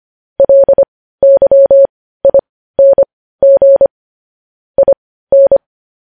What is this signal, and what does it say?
Morse code, heard as a keyed tone, about 25 wpm, spaced as for 15 wpm, 550 Hz. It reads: LYING IN
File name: podcast / News headlines-1x-25x15